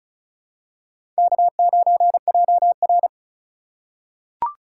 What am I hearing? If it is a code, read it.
Morse code, heard as a keyed tone, about 34 wpm, 700 Hz. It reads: K9JR
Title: USA Call Signs - 2x Character Spacing - 35wpm